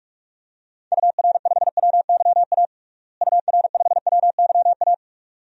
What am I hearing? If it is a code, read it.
Morse code, heard as a keyed tone, about 41 wpm, 700 Hz. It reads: UR5WYA UR5WYA
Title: DXCC-100 Call Signs - 6 Characters - Repeated 2x - 45wpm